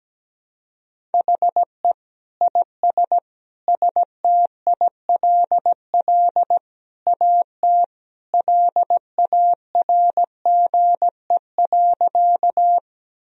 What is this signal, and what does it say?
Morse code, heard as a keyed tone, about 17 wpm, 700 Hz. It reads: HE IS STILL AT LARGE.
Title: Sentences from Top 1000 Words - Review Part 1 - 17wpm